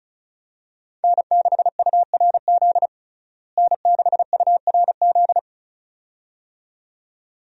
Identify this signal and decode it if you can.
Morse code, heard as a keyed tone, about 36 wpm, 700 Hz. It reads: N6URZ N6URZ
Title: USA Call Signs - 5 Characters - Repeated 2x - 35wpm